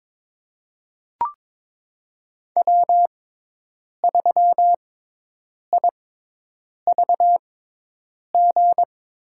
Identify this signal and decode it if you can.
Morse code, heard as a keyed tone, about 22 wpm, spaced as for 8 wpm, 700 Hz. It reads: W3IVG
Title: USA Call Signs - 5 Characters - 6x Character Spacing - 22wpm